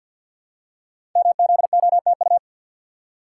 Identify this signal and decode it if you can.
Morse code, heard as a keyed tone, about 53 wpm, 700 Hz. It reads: MZOTU
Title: Code Groups - 5 Letters - 50wpm